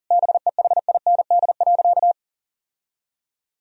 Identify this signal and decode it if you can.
Morse code, heard as a keyed tone, about 40 wpm, 700 Hz. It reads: BEHIND.
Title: Sentences from Top 1000 Words - Review Part 1 - 40wpm